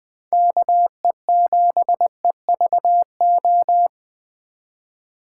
Morse code, heard as a keyed tone, about 20 wpm, 700 Hz. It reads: KE7EVO
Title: Speed Racing - USA 2x3 Call Signs 17wpm